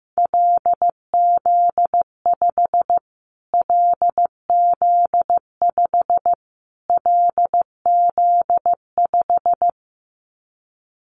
Morse code, heard as a keyed tone, about 15 wpm, 700 Hz. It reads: LZ5 LZ5 LZ5
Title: Callsign Prefixes for Top 25 DXCC - Repeated 3x - 15wpm